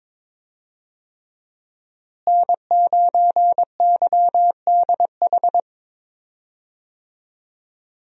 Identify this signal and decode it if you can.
Morse code, heard as a keyed tone, about 22 wpm, 700 Hz. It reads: N9YDH
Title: USA Call Signs - 5 Characters - Repeated 3x - 22wpm